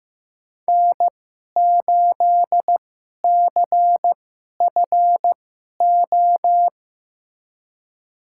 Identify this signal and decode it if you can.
Morse code, heard as a keyed tone, about 15 wpm, spaced as for 11 wpm, 700 Hz. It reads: N8CFO
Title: USA Call Signs - 5 Characters - 2x Character Spacing - 15wpm